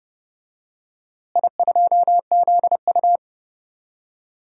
Morse code, heard as a keyed tone, about 30 wpm, 700 Hz. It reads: I2ZU
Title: DXCC-100 Call Signs - 4 Characters - Repeated 3x - 30wpm